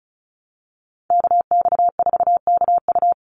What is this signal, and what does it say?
Morse code, heard as a keyed tone, about 34 wpm, 700 Hz. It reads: KX4KU
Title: Call Signs - 35wpm